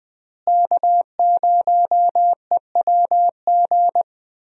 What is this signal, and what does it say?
Morse code, heard as a keyed tone, about 20 wpm, 700 Hz. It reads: K0EWG
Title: USA Call Signs - 5 Characters - Repeated 3x - 20wpm